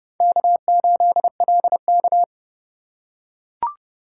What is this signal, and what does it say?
Morse code, heard as a keyed tone, about 31 wpm, 700 Hz. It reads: K8LK
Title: USA Call Signs - 4 Characters - 3x Character Spacing - 30wpm